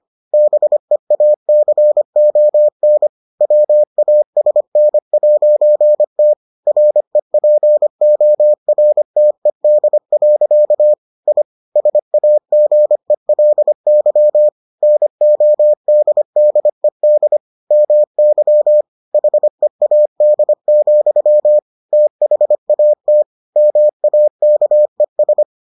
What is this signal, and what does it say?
Morse code, heard as a keyed tone, about 25 wpm, 600 Hz. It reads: BEACON WASN'T REPORTED. I SAGELY NODDED MY HEAD, THAT MAKES